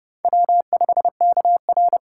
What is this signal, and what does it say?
Morse code, heard as a keyed tone, about 27 wpm, 700 Hz. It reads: W5KR